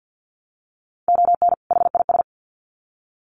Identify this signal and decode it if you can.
Morse code, heard as a keyed tone, about 50 wpm, 700 Hz. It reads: ON HIS
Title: Sets of 2 Words 50wpm